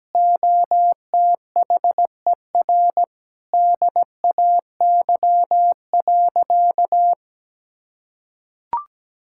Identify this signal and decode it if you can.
Morse code, heard as a keyed tone, about 17 wpm, 700 Hz. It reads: OTHER DAY.
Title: Sentences from Top 1000 Words - Review Part 5 - 17wpm